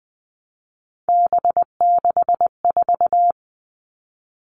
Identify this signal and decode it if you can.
Morse code, heard as a keyed tone, about 20 wpm, 700 Hz. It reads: B64
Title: All Cut Numbers - 3-Digit 20wpm